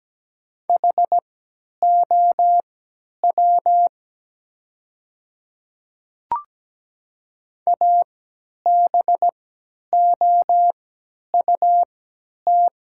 Morse code, heard as a keyed tone, about 17 wpm, spaced as for 10 wpm, 700 Hz. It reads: HOW ABOUT
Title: 3x Character Spacing - Top 100 Words - 17wpm